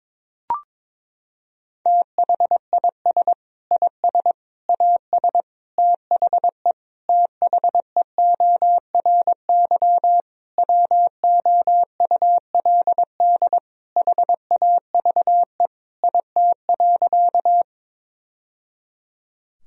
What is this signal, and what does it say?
Morse code, heard as a keyed tone, about 22 wpm, 700 Hz. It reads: THIS IS AS THE THEORY WOULD HAVE IT.
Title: Sentences from Top 1000 Words - Review Part 3 - 22wpm